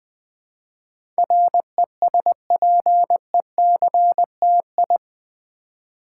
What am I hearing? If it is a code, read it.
Morse code, heard as a keyed tone, about 20 wpm, 700 Hz. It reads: RESPECTI
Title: Build A Word - 4 Letters At A Time - 12-Letter Words - 20wpm